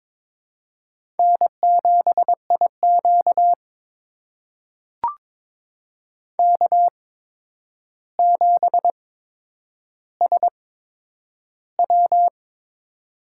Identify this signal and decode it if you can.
Morse code, heard as a keyed tone, about 22 wpm, spaced as for 9 wpm, 700 Hz. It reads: N7IQ K7SW
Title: USA Call Signs - 4 Characters - 8x Character Spacing - 22wpm